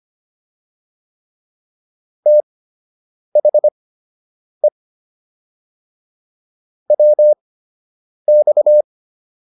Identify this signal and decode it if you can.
Morse code, heard as a keyed tone, about 25 wpm, spaced as for 8 wpm, 600 Hz. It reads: THE WX